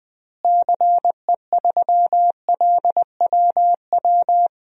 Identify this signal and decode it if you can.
Morse code, heard as a keyed tone, about 21 wpm, 700 Hz. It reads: CE3LWW
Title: DXCC-100 Call Signs - 6 Characters - Repeated 3x - 20wpm